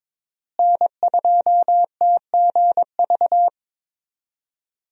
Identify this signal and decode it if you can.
Morse code, heard as a keyed tone, about 22 wpm, 700 Hz. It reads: N2TGV
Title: USA Call Signs - 5 Characters - Repeated 3x - 22wpm